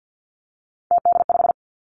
Morse code, heard as a keyed tone, about 50 wpm, 700 Hz. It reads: TD5
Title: Three Challenging Letter-Number Combinations 50wpm